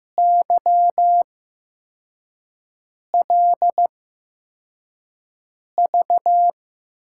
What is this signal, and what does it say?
Morse code, heard as a keyed tone, about 15 wpm, spaced as for 4 wpm, 700 Hz. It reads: YLV
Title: USA Call Signs - 5 Characters - 8x Character Spacing - 15wpm